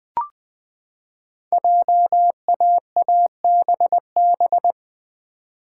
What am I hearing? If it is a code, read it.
Morse code, heard as a keyed tone, about 20 wpm, 700 Hz. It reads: JAABB
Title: Code Groups - 5 Letters - 20wpm